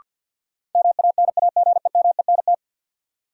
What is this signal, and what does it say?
Morse code, heard as a keyed tone, about 52 wpm, 700 Hz. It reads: MANAGEMENT
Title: Sentences from Top 1200 Words - New Word Drill Alt - 50wpm